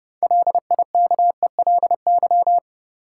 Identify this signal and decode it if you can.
Morse code, heard as a keyed tone, about 30 wpm, 700 Hz. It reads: LIKELY
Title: Sentences from Top 600 Words - New Word Drill - 30wpm